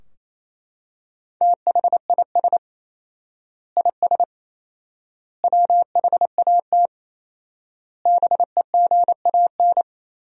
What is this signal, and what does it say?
Morse code, heard as a keyed tone, about 28 wpm, 700 Hz. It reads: THIS IS WHAT BEGAN
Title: 4x Word-Spacing - Sentences from Top 300 Words - 28wpm